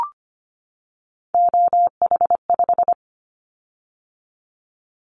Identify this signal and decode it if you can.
Morse code, heard as a keyed tone, about 25 wpm, 700 Hz. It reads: OH5
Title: Callsign Prefixes for Top 50 DXCC - 25wpm